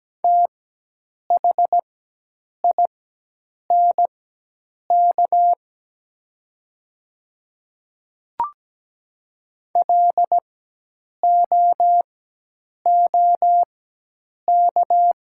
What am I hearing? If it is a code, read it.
Morse code, heard as a keyed tone, about 17 wpm, spaced as for 8 wpm, 700 Hz. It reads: THINK LOOK
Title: 4x Character Spacing - Top 300 Words - 17wpm